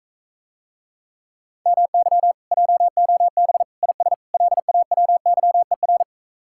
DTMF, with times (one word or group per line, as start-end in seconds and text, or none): none